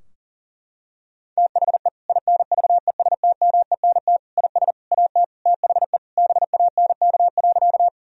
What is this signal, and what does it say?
Morse code, heard as a keyed tone, about 40 wpm, 700 Hz. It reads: THE INVESTMENT IS AT THE BANK.
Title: Sentences from Top 1000 Words - Review Part 4 - 40wpm